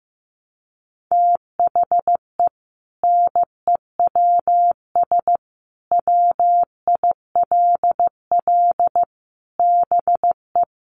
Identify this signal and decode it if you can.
Morse code, heard as a keyed tone, about 15 wpm, 700 Hz. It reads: THE NEWS WILL BE
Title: Sentences from Top 1200 Words 15wpm